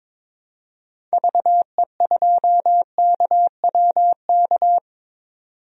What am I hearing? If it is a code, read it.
Morse code, heard as a keyed tone, about 22 wpm, 700 Hz. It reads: VE2KWK
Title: DXCC-100 Call Signs - 6 Characters - 22wpm